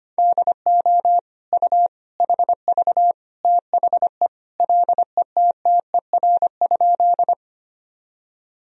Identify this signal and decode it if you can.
Morse code, heard as a keyed tone, about 25 wpm, 700 Hz. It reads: DO U HV THE LETTER?
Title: Sentences from Top 300 Words and Abbreviations 25wpm